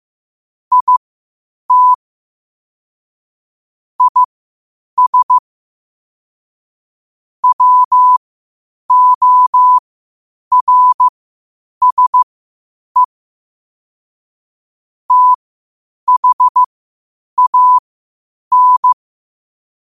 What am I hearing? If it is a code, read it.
Morse code, heard as a keyed tone, about 15 wpm, spaced as for 8 wpm, 1000 Hz. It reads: IT IS WORSE THAN
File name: qotdfiles / 8wpm